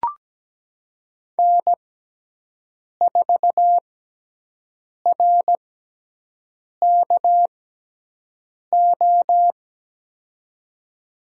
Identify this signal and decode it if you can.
Morse code, heard as a keyed tone, about 17 wpm, spaced as for 6 wpm, 700 Hz. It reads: N4RKO
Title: USA Call Signs - 5 Characters - 6x Character Spacing - 17wpm